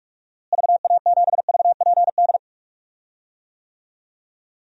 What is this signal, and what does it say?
Morse code, heard as a keyed tone, about 46 wpm, 700 Hz. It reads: VA7VPD
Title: DXCC-100 Call Signs - 6 Characters - Repeated 3x - 45wpm